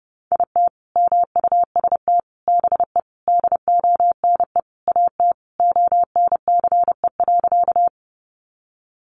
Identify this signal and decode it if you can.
Morse code, heard as a keyed tone, about 30 wpm, 700 Hz. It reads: IT MUST BE DONE AT ONCE.
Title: Sentences from Top 500 Words 30wpm